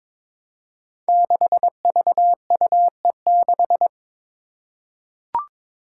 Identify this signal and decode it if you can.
Morse code, heard as a keyed tone, about 22 wpm, 700 Hz. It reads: 6VUE6
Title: Five Challenging Letter-Number Combinations 22wpm